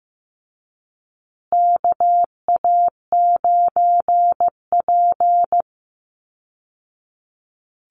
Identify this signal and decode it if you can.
Morse code, heard as a keyed tone, about 15 wpm, 700 Hz. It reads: KA9P
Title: Call Signs Repeated 2x - 15wpm